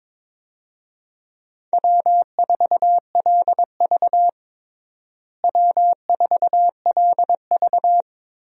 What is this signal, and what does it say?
Morse code, heard as a keyed tone, about 23 wpm, 700 Hz. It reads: W4LV W4LV
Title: USA Call Signs - 4 Characters - Repeated 2x - 22wpm